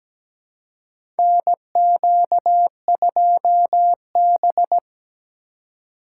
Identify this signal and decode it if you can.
Morse code, heard as a keyed tone, about 17 wpm, 700 Hz. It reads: NQ2B
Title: USA Call Signs - 4 Characters - 4x Character Spacing - 17wpm